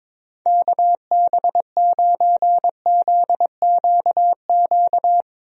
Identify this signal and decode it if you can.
Morse code, heard as a keyed tone, about 22 wpm, 700 Hz. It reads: KB9ZQQ
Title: USA Call Signs - 6 Characters - Repeated 2x - 22wpm